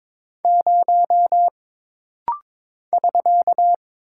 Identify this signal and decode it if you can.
Morse code, heard as a keyed tone, about 22 wpm, 700 Hz. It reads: 0 <SK>
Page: This practice set is composed of Letters, Numbers, common punctuations, and common prosigns.